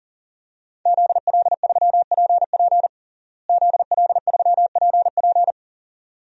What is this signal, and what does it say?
Morse code, heard as a keyed tone, about 40 wpm, 700 Hz. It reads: ZL3PP ZL3PP
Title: DXCC-100 Call Signs - 5 Characters - Repeated 2x - 40wpm